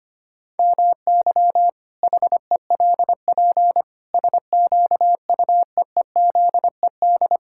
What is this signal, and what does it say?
Morse code, heard as a keyed tone, about 25 wpm, 700 Hz. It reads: MY HELP SQUEEZED
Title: Pangram Challenge v2 - 25wpm